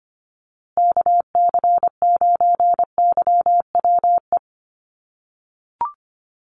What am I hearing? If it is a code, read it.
Morse code, heard as a keyed tone, about 25 wpm, 700 Hz. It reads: KC9YWE
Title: USA Call Signs Repeated 3x 25wpm